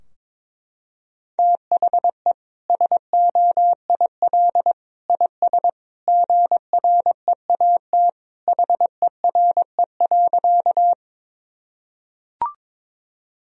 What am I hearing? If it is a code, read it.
Morse code, heard as a keyed tone, about 22 wpm, 700 Hz. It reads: THE SOIL IS GREAT HERE.
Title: Sentences from Top 1000 Words - Review Part 4 - 22wpm